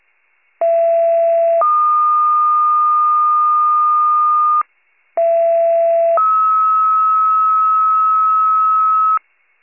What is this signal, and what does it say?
Two-tone page, heard as a one-second tone, then a three-second tone, about 670 then 1220 Hz, about 670 then 1290 Hz. It is heard from 0.6 to 9.2 s.
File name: wcalert